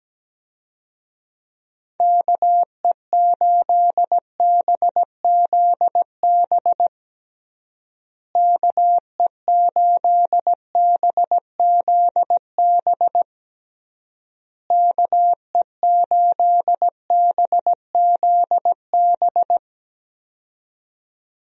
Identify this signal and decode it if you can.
Morse code, heard as a keyed tone, about 17 wpm, 700 Hz. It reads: KE8BZB KE8BZB KE8BZB
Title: USA Call Signs - 6 Characters - Repeated 3x - 17wpm